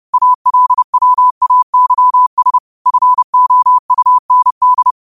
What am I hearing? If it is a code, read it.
Morse code, heard as a keyed tone, about 30 wpm, 1000 Hz. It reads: ALWAYS FOUND